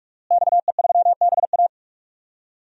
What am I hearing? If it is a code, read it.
Morse code, heard as a keyed tone, about 46 wpm, 700 Hz. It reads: XE3BA
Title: DXCC-100 Call Signs - 5 Characters - Repeated 3x - 45wpm